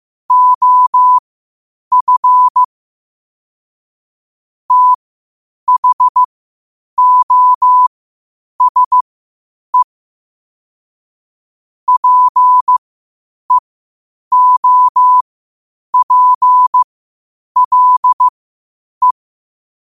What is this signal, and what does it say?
Morse code, heard as a keyed tone, about 15 wpm, spaced as for 8 wpm, 1000 Hz. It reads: OF THOSE PEOPLE
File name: qotdfiles / 8wpm